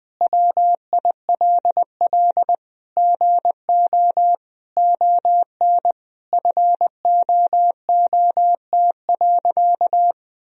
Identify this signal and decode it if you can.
Morse code, heard as a keyed tone, about 20 wpm, 700 Hz. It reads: WILL GO ON FOOT.
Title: Sentences from Top 1000 Words - Review Part 3 - 20wpm